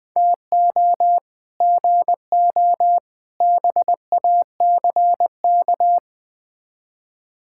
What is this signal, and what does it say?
Morse code, heard as a keyed tone, about 20 wpm, 690 Hz. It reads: TO GO BACK
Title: Build A Phrase - 4-Word Phrases - 20wpm